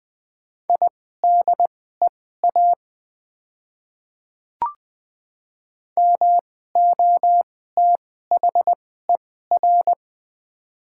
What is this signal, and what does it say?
Morse code, heard as a keyed tone, about 20 wpm, spaced as for 15 wpm, 700 Hz. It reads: IDEA MOTHER
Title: 2x Character Spacing - Top 300 Words - 20wpm